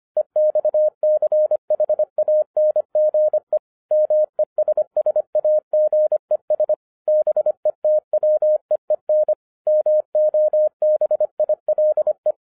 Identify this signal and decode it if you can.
Morse code, heard as a keyed tone, about 25 wpm, 610 Hz. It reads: EXCHANGE MESSAGES BETWEEN MOBILE
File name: 2011-10-mp3 / n09-CW TO PS2 CONVERTER 25wpm